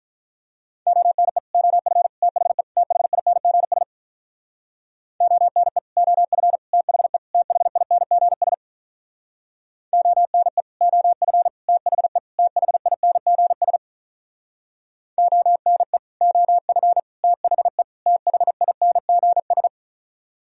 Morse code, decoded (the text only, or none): ONE OF THE THINGS ONE OF THE THINGS ONE OF THE THINGS ONE OF THE THINGS